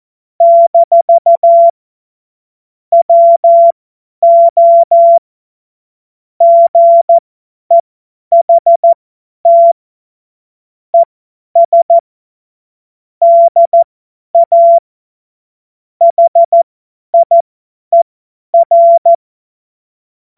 Morse code, heard as a keyed tone, about 14 wpm, spaced as for 10 wpm, 680 Hz. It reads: - WO GEHT ES DA HIER
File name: Witz-30-September-2025-50-F70